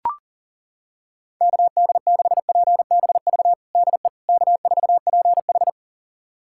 Morse code, heard as a keyed tone, about 40 wpm, 700 Hz. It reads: KD6PBV DE K4PH
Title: USA Callsign DE Callsign 40wpm